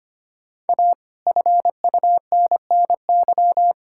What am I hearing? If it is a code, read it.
Morse code, heard as a keyed tone, about 25 wpm, 700 Hz. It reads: A FUNNY
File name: Sets of 9 Words - 25wpm